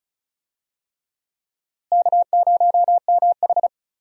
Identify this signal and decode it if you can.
Morse code, heard as a keyed tone, about 36 wpm, 700 Hz. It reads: K0MH
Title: USA Call Signs - 4 Characters - Repeated 3x - 35wpm